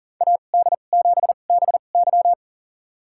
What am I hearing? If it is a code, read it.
Morse code, heard as a keyed tone, about 39 wpm, spaced as for 29 wpm, 700 Hz. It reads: AD7BY
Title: USA Call Signs - 5 Characters - 2x Character Spacing - 40wpm